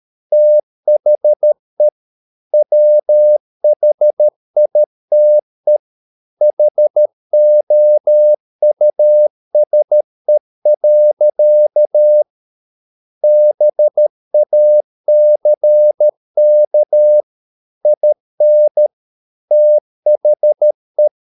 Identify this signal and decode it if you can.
Morse code, heard as a keyed tone, about 13 wpm, 600 Hz. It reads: THE WHITE HOUSE. BACK IN THE